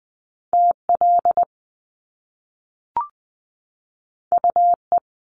Morse code, heard as a keyed tone, about 20 wpm, 700 Hz. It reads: TL UE
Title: Top Two Letter Combinations in Words 20wpm